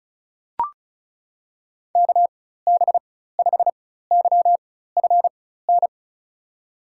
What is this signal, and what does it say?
Morse code, heard as a keyed tone, about 36 wpm, spaced as for 16 wpm, 700 Hz. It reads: KB5YFN
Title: USA Call Signs - 6 Characters - 4x Character Spacing - 35wpm